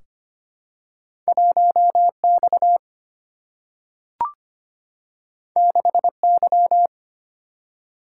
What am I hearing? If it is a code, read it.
Morse code, heard as a keyed tone, about 25 wpm, 700 Hz. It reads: 1X 6Y
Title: Single-Letter Single-Number 25wpm